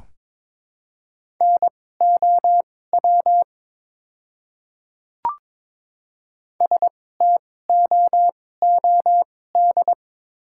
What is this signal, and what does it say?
Morse code, heard as a keyed tone, about 22 wpm, spaced as for 16 wpm, 700 Hz. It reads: NOW STOOD
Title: 2x Character Spacing - Top 500 Words - 22wpm